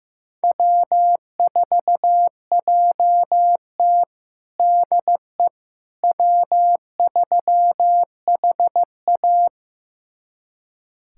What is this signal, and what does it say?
Morse code, heard as a keyed tone, about 15 wpm, 700 Hz. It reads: W4JT DE W3HA
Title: USA Callsign DE Callsign - 4 Characters - 15wpm